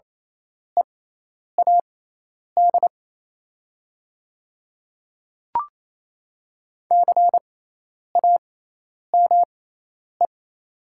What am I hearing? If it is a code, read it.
Morse code, heard as a keyed tone, about 27 wpm, spaced as for 10 wpm, 700 Hz. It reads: EAD CAME
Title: 6x Character Spacing - Top 200 Words - 28wpm